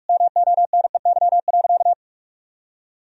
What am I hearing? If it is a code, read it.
Morse code, heard as a keyed tone, about 46 wpm, 700 Hz. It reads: MONEY.
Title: Sentences from Top 1000 Words - Review Part 4 - 45wpm